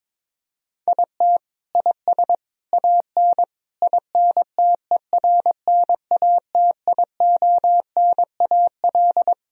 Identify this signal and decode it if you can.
Morse code, heard as a keyed tone, about 22 wpm, 700 Hz. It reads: IT IS AN INTERNATIONAL
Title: Sentences from Top 1000 Words - Review Part 4 - 22wpm